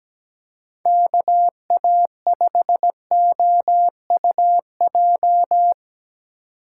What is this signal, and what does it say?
Morse code, heard as a keyed tone, about 17 wpm, 700 Hz. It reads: KA5OUJ
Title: USA Call Signs - 4x Character Spacing - 17wpm